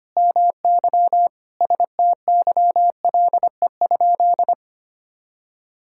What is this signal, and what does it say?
Morse code, heard as a keyed tone, about 25 wpm, 700 Hz. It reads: MY STYLE?
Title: Sentences from Top 1000 Words - Review Part 5 - 25wpm